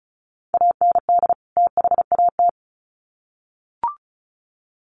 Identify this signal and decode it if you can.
Morse code, heard as a keyed tone, about 34 wpm, 700 Hz. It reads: AND THAT
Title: Sets of 2 Words 35wpm